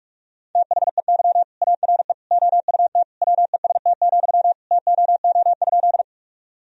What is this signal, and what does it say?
Morse code, heard as a keyed tone, about 47 wpm, 700 Hz. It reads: THEY ARE OUT WEST, TOO?
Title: Sentences from Top 1000 Words - Review Part 2 - 45wpm